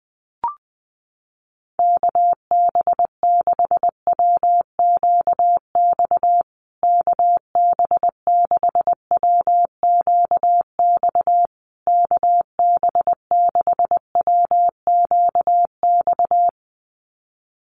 Morse code, heard as a keyed tone, about 20 wpm, 700 Hz. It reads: KB6WQX KB6WQX KB6WQX
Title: USA Call Signs Repeated 3x 20wpm